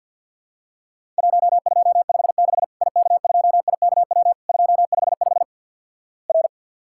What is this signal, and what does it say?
Morse code, heard as a keyed tone, about 50 wpm, 700 Hz. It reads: 1256 IK2IKW 255 R
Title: Worked All Europe DX Contest - QTC Exchange with Repeats - 50wpm